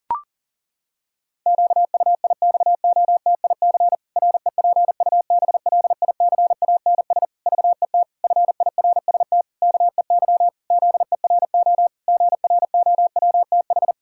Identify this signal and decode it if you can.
Morse code, heard as a keyed tone, about 40 wpm, 700 Hz. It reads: QUIXOTIC REPUBLICANS VET FIRST KEY ZERO GROWTH
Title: Pangram Challenge v2 - 40wpm